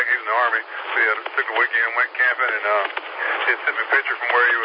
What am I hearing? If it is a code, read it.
Emergency Alert System (SAME) tones, heard as harsh buzzing bursts, with speech. It is heard from 0.5 to 4.2 s.